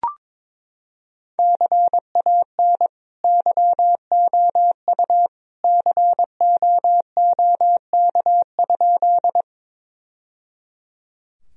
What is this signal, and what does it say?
Morse code, heard as a keyed tone, about 22 wpm, 700 Hz. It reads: CAN YOU COOK?
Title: Sentences from Top 1000 Words - Review Part 4 - 22wpm